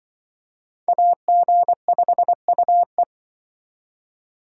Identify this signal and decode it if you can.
Morse code, heard as a keyed tone, about 25 wpm, 700 Hz. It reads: AG5UE